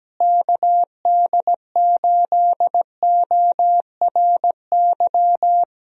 Morse code, heard as a keyed tone, about 17 wpm, 700 Hz. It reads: KD8ORY
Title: USA Call Signs - 6 Characters - Repeated 2x - 17wpm